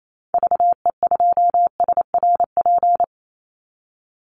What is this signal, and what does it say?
Morse code, heard as a keyed tone, about 28 wpm, 700 Hz. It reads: VE2SRP
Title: Call Signs - 28wpm